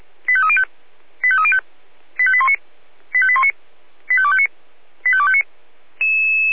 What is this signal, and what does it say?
Five-tone selective call (ZVEI), heard as quick runs of five tones, tones of about 70 ms. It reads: 85305 85305 85819 85819 85259 85259